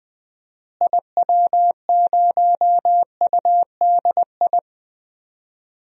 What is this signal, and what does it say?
Morse code, heard as a keyed tone, about 20 wpm, 700 Hz. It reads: IW0UDI
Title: DXCC-100 Call Signs - 6 Characters - Repeated 2x - 20wpm